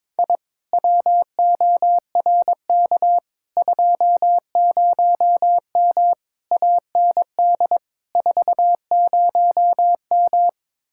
Morse code, heard as a keyed tone, about 22 wpm, 700 Hz. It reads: I WORK 20M AND 40M
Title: Speed-Racing Common QSO Phrases - 22wpm